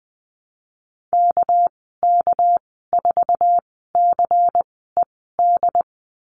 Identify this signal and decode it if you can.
Morse code, heard as a keyed tone, about 20 wpm, spaced as for 15 wpm, 700 Hz. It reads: KK4CED
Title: 219 Lesson 35g - Number 0 - Call Signs 20wpm